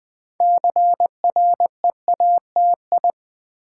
Morse code, heard as a keyed tone, about 20 wpm, 700 Hz. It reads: CREATI
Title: Build A Word - 2 Letters At A Time - 8-Letter Words - 20wpm